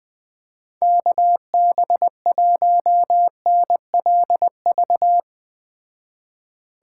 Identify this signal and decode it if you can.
Morse code, heard as a keyed tone, about 20 wpm, 700 Hz. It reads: KB1NLV